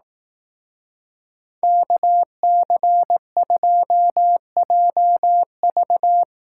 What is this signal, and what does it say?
Morse code, heard as a keyed tone, about 18 wpm, 700 Hz. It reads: KC2JV
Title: Speed Racing - USA 2x2 Call Signs 15wpm